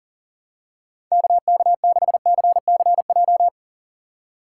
Morse code, heard as a keyed tone, about 40 wpm, 700 Hz. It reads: KK6CCJ
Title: USA Call Signs - 6 Characters - 6x Character Spacing - 40wpm